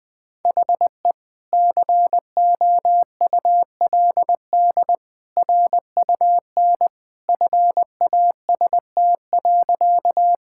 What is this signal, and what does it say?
Morse code, heard as a keyed tone, about 20 wpm, 700 Hz. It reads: HE COULD RUN FAST.
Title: Sentences with 2-Word Phrases - Spoken Beginning - 20wpm